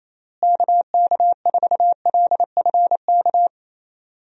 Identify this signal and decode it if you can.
Morse code, heard as a keyed tone, about 27 wpm, 700 Hz. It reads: KK4LFK
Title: USA Call Signs - 6 Characters - Repeated 3x - 28wpm